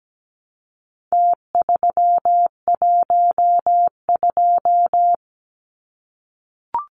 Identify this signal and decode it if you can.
Morse code, heard as a keyed tone, about 17 wpm, 700 Hz. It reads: T312
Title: Numbers - Contest Style - 4-Digit 17wpm